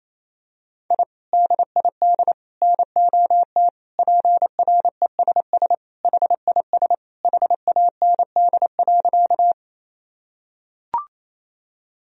Morse code, heard as a keyed tone, about 28 wpm, 700 Hz. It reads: I DID NOT PRESS HIS HAND.
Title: Sentences from Top 1000 Words - Review Part 1 - 28wpm